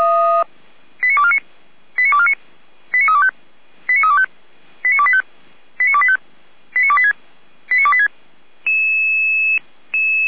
Five-tone selective call (ZVEI), heard as quick runs of five tones, tones of about 80 ms. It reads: 89249 89249 89325 89325 89285 89285 89286 89286